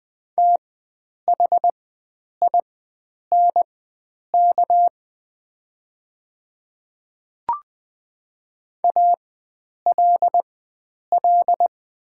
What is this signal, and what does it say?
Morse code, heard as a keyed tone, about 20 wpm, spaced as for 9 wpm, 700 Hz. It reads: THINK ALL
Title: 4x Character Spacing - Top 300 Words - 20wpm